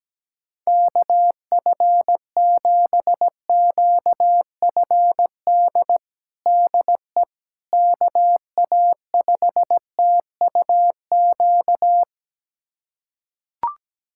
Morse code, heard as a keyed tone, about 17 wpm, 700 Hz. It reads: KF7QFD DE KA5TUQ
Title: USA Callsign DE Callsign - 6 Characters - 17wpm